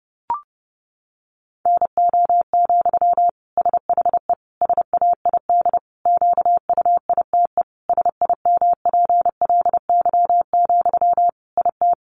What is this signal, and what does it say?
Morse code, heard as a keyed tone, about 30 wpm, 700 Hz. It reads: NO, SHE SAID QUITE SIMPLY, IT
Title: Sentences from Top 800 Words 30wpm